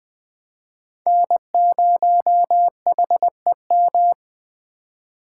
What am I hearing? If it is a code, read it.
Morse code, heard as a keyed tone, about 24 wpm, 700 Hz. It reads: N0HEM
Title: Speed Racing - USA 1x3 Call Signs 20wpm